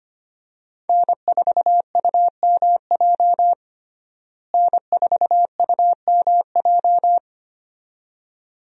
Morse code, heard as a keyed tone, about 25 wpm, 700 Hz. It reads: N4UMJ N4UMJ
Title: USA Call Signs - 5 Characters - Repeated 2x - 25wpm